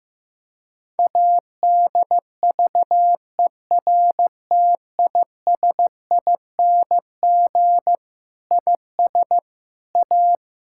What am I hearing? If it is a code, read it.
Morse code, heard as a keyed tone, about 15 wpm, 700 Hz. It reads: ADVERTISING IS A
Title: Sentences from Top 1000 Words - Review Part 6 - 15wpm